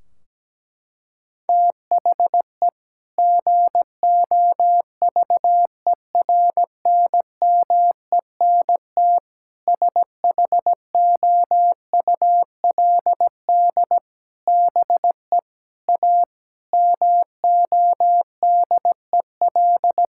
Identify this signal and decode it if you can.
Morse code, heard as a keyed tone, about 17 wpm, 700 Hz. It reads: THE GOVERNMENT SHOULD BE A MODEL
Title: Sentences from Top 1600 Words 17wpm